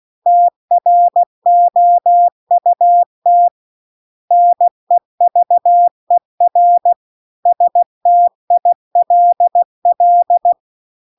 Morse code, heard as a keyed tone, about 16 wpm, 700 Hz. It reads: TROUT NEVER STILL
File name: CW-The-Big-Twoheaded-River-2-16WPM